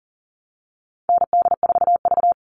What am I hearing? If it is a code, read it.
Morse code, heard as a keyed tone, about 40 wpm, 700 Hz. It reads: ND4V
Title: All Cut Numbers - 4-Digit 40wpm